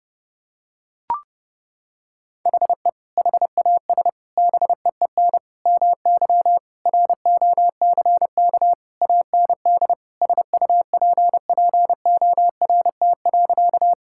Morse code, heard as a keyed tone, about 30 wpm, 700 Hz. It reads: HE HAS BEEN MY ROCK AND SUPPORT.
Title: Sentences from Top 1000 Words - Review Part 5 - 30wpm